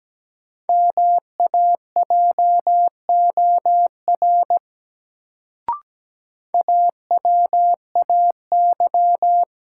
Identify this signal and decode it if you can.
Morse code, heard as a keyed tone, about 17 wpm, 700 Hz. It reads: MAJOR AWAY
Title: Sentences from Top 600 Words - New Word Drill - 17wpm